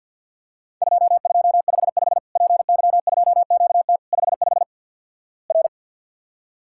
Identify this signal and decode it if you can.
Morse code, heard as a keyed tone, about 50 wpm, 700 Hz. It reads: 2255 PY2QT 55 R
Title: Worked All Europe DX Contest - QTC Exchange - 50wpm